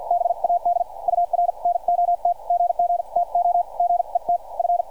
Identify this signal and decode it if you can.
Morse code, heard as a keyed tone, about 50 wpm, spaced as for 43 wpm, 700 Hz, with radio noise. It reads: BAD UR NOT GG TO GET F